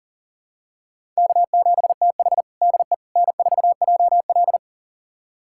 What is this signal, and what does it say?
Morse code, heard as a keyed tone, about 40 wpm, 700 Hz. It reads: K7TH DE N4JL
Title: USA Callsign DE Callsign - 4 Characters - 40wpm